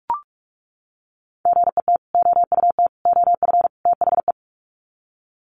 Morse code, heard as a keyed tone, about 44 wpm, 700 Hz. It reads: GET OUT OF THE
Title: Sets of 4 Words 45wpm